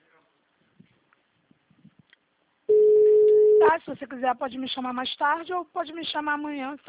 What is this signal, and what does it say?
Ringback tone (Europe, 425 Hz), with speech.
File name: out- -2023- - - .10098